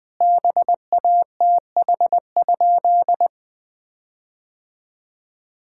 Morse code, heard as a keyed tone, about 20 wpm, 700 Hz. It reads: BATH?
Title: Sentences from Top 1600 Words 20wpm